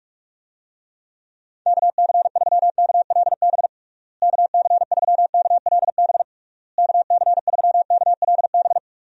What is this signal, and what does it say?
Morse code, heard as a keyed tone, about 46 wpm, 700 Hz. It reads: KC3KLB KC3KLB KC3KLB
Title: USA Call Signs - 6 Characters - Repeated 3x - 45wpm